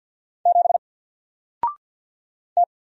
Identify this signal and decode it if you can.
Morse code, heard as a keyed tone, about 48 wpm, 700 Hz. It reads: 7 T
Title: Single Letter-Number Alt 50wpm